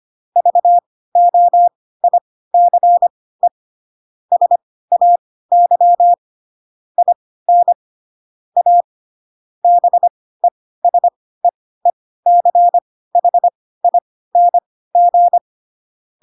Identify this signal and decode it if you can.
Morse code, heard as a keyed tone, about 25 wpm, spaced as for 16 wpm, 700 Hz. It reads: VOICE SAY IN A BESEECHING
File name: Pinocchio / Pinocchio ch1 10 25 ik1hge